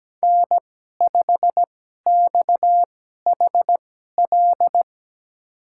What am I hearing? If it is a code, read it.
Morse code, heard as a keyed tone, about 17 wpm, spaced as for 12 wpm, 700 Hz. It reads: N5XHL
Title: USA Call Signs - 5 Characters - 2x Character Spacing - 17wpm